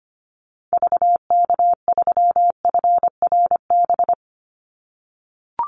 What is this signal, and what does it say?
Morse code, heard as a keyed tone, about 25 wpm, 700 Hz. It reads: VK3FRB
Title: USA Call Signs 25wpm